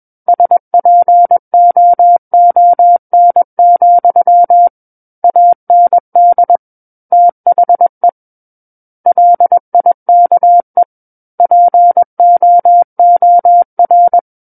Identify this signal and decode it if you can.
Morse code, heard as a keyed tone, about 21 wpm, 700 Hz. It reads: SPOON, AND THE LIKE POOR
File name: Book One / 12-Chapter 12 - 21 WPM